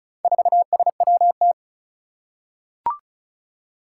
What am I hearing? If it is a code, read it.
Morse code, heard as a keyed tone, about 36 wpm, 700 Hz. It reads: 4SWT
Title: Four Challenging Letter-Number Combinations 35wpm